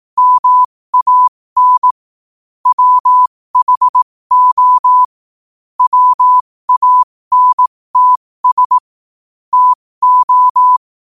Morse code, heard as a keyed tone, about 18 wpm, spaced as for 15 wpm, 1000 Hz. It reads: MAN WHO WANTS TO
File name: qotdfiles / 15wpm